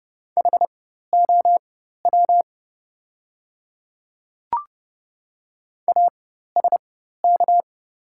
Morse code, heard as a keyed tone, about 30 wpm, spaced as for 14 wpm, 700 Hz. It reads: HOW ASK
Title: 4x Character Spacing - Top 300 Words - 30wpm